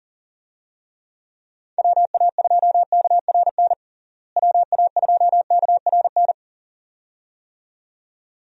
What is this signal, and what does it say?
Morse code, heard as a keyed tone, about 40 wpm, 700 Hz. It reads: WA2KRN WA2KRN
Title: USA Call Signs - 6 Characters - Repeated 2x - 40wpm